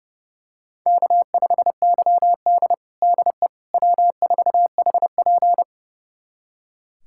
Morse code, heard as a keyed tone, about 30 wpm, 700 Hz. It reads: K5YD DE W4HP
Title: USA Callsign DE Callsign - 4 Characters - 30wpm